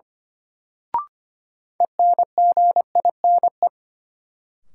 Morse code, heard as a keyed tone, about 25 wpm, 700 Hz. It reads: ENGINE
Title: Sentences from Top 1500 Words - New Word Drill - 25wpm